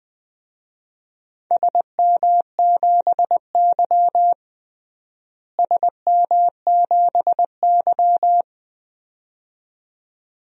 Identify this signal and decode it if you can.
Morse code, heard as a keyed tone, about 21 wpm, 700 Hz. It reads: SM7Y SM7Y
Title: DXCC-100 Call Signs - 4 Characters - Repeated 2x - 20wpm